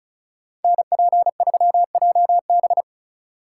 Morse code, heard as a keyed tone, about 43 wpm, 700 Hz. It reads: NP3JB
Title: Speed Racing - USA Call Signs 35wpm